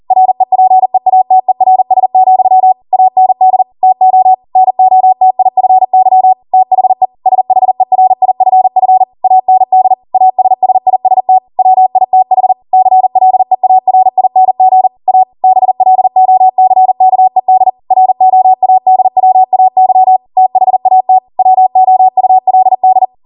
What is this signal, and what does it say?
Morse code, heard as a keyed tone, about 40 wpm, 750 Hz. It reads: REPEATERS, AND TO NOTIFY THE SHERIFF AND ASSIST WITH CLEARING A BLOCKED ROADWAY THAT WOULD